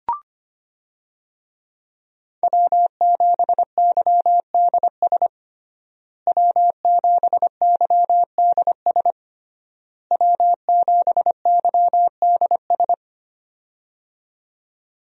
Morse code, heard as a keyed tone, about 25 wpm, 700 Hz. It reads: W7YDS W7YDS W7YDS
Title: USA Call Signs - 5 Characters - Repeated 3x - 25wpm